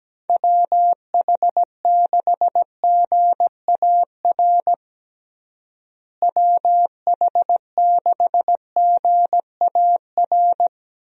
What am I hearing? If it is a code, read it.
Morse code, heard as a keyed tone, about 17 wpm, 700 Hz. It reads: WH6GAR WH6GAR
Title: USA Call Signs - 6 Characters - Repeated 3x - 17wpm